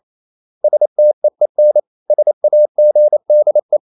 Morse code, heard as a keyed tone, about 28 wpm, 600 Hz. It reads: STEEN SAGDE
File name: Ferie QRQ 28wpm